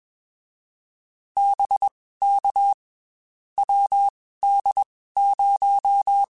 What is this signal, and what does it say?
Morse code, heard as a keyed tone, about 21 wpm, spaced as for 15 wpm, 800 Hz. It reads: BK WD0
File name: QSOs / QSOs-mp3s / WD0FBT 1@20